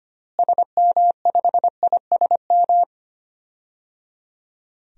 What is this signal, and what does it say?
Morse code, heard as a keyed tone, about 25 wpm, 690 Hz. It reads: SM5ISM
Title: DXCC-100 Call Signs - 6 Characters - Repeated 3x - 25wpm